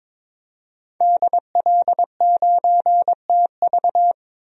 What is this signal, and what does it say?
Morse code, heard as a keyed tone, about 22 wpm, 700 Hz. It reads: DL9TV
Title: DXCC-100 Call Signs - 5 Characters - 22wpm